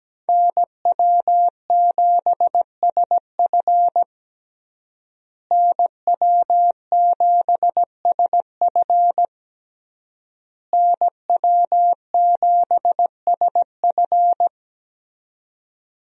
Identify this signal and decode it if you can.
Morse code, heard as a keyed tone, about 17 wpm, 700 Hz. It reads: NW7SF NW7SF NW7SF
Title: USA Call Signs - 5 Characters - Repeated 3x - 17wpm